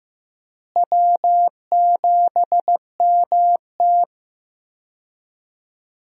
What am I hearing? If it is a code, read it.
Morse code, heard as a keyed tone, about 15 wpm, 700 Hz. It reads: W7MT
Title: USA Call Signs - 4 Characters - Repeated 2x - 15wpm